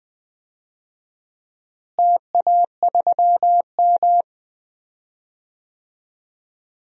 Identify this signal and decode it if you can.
Morse code, heard as a keyed tone, about 20 wpm, 700 Hz. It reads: TA3M
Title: DXCC-100 Call Signs - 4 Characters - Repeated 3x - 20wpm